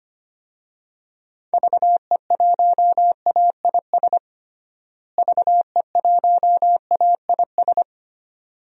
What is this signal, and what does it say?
Morse code, heard as a keyed tone, about 25 wpm, 700 Hz. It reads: VE1AIS VE1AIS
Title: DXCC-100 Call Signs - 6 Characters - Repeated 2x - 25wpm